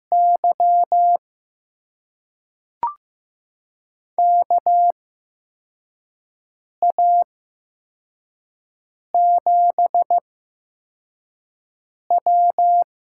Morse code, heard as a keyed tone, about 15 wpm, spaced as for 4 wpm, 700 Hz. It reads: Y KA7W
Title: USA Call Signs - 6 Characters - 8x Character Spacing - 15wpm